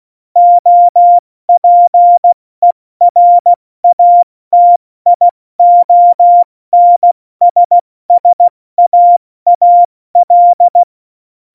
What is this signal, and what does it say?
Morse code, heard as a keyed tone, about 16 wpm, 700 Hz. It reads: OPERATIONSSAAL
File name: Morsekurs / Witze / Witz 15. September 2020, 70 F80